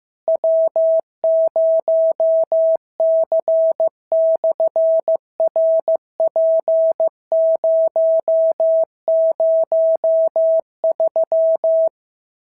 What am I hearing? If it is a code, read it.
Morse code, heard as a keyed tone, about 15 wpm, 650 Hz. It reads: W0C/RP003
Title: SOTA QSO Practice 15wpm